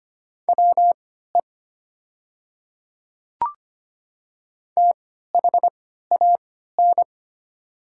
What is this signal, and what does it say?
Morse code, heard as a keyed tone, about 25 wpm, spaced as for 14 wpm, 700 Hz. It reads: WE THAN
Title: 3x Character Spacing - Top 100 Words - 25wpm